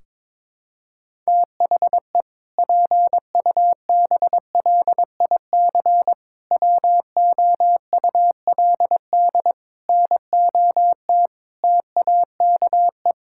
Morse code, heard as a keyed tone, about 22 wpm, 700 Hz. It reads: THE PUBLIC WOULD NOT TAKE
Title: Sentences from Top 1600 Words 22wpm